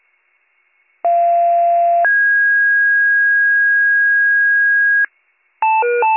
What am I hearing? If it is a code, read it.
Two-tone page, heard as a one-second tone, then a three-second tone, about 690 then 1650 Hz. It is heard from 1.0 to 5.1 s.